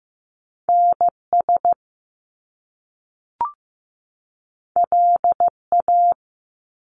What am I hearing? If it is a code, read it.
Morse code, heard as a keyed tone, about 15 wpm, 700 Hz. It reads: NS LA
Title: Top Two Letter Combinations in Words 15wpm